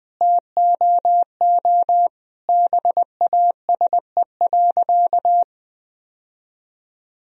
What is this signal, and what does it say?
Morse code, heard as a keyed tone, about 20 wpm, 700 Hz. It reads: TOO BASE.
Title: Sentences from Top 1000 Words - Review Part 1 - 20wpm